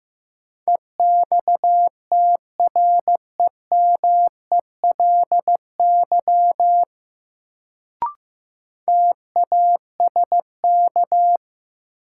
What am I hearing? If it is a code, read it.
Morse code, heard as a keyed tone, about 15 wpm, 700 Hz. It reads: EXTREMELY TASK
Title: Sentences from Top 1000 Words - Review Part 7 - New Word Drill - 15wpm